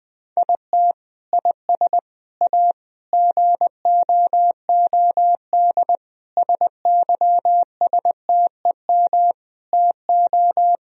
Morse code, heard as a keyed tone, about 20 wpm, 700 Hz. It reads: IT IS A GOOD SYSTEM TO
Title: Sentences from Top 1000 Words - Review Part 3 - 20wpm